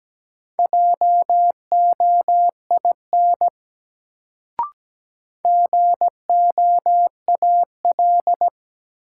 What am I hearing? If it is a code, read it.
Morse code, heard as a keyed tone, about 17 wpm, 700 Hz. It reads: JOIN GOAL
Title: Sentences from Top 1400 Words - New Word Drill Alt - 17wpm